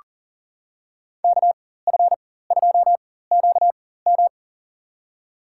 Morse code, heard as a keyed tone, about 38 wpm, spaced as for 18 wpm, 700 Hz. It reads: KF2QM
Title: USA Call Signs - 5 Characters - 4x Character Spacing - 40wpm